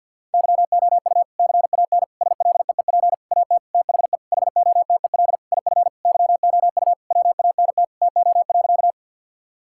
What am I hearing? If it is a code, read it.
Morse code, heard as a keyed tone, about 50 wpm, 700 Hz. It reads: YOU CAN SLEEP AT THE HOTEL IF YOU WANT TO.
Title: Sentences from Top 1600 Words 50wpm